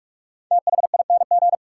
Morse code, heard as a keyed tone, about 47 wpm, 700 Hz. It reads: THING
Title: Sentences from Top 100 Words - New Word Drill - 45wpm